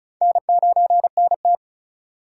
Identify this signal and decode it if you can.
Morse code, heard as a keyed tone, about 48 wpm, spaced as for 42 wpm, 700 Hz. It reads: N9NT
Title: Speed Racing - USA 1x2 Call Signs 35wpm